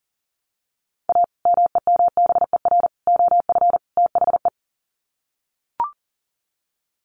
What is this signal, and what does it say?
Morse code, heard as a keyed tone, about 40 wpm, 700 Hz. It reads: A MEMBER OF THE
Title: Sets of 4 Words 40wpm